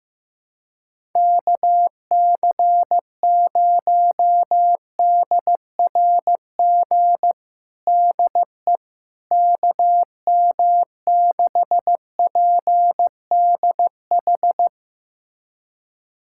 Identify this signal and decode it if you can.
Morse code, heard as a keyed tone, about 15 wpm, 700 Hz. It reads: KC0DRG DE KM6PDH
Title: USA Callsign DE Callsign - 6 Characters - 15wpm